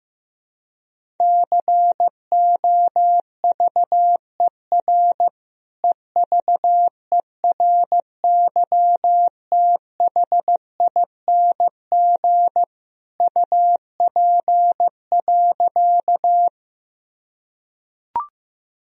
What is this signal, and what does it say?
Morse code, heard as a keyed tone, about 16 wpm, 700 Hz. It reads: COVER EVERYTHING UP.
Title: Sentences from Top 1000 Words - Review Part 1 - 15wpm